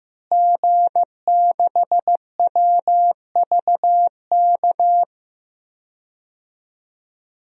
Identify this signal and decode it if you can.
Morse code, heard as a keyed tone, about 16 wpm, 700 Hz. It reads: G6WVK
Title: DXCC-100 Call Signs - 5 Characters - Repeated 3x - 15wpm